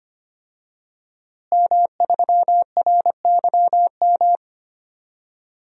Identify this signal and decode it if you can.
Morse code, heard as a keyed tone, about 25 wpm, 700 Hz. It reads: M3RYM
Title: DXCC-100 Call Signs - 5 Characters - Repeated 2x - 25wpm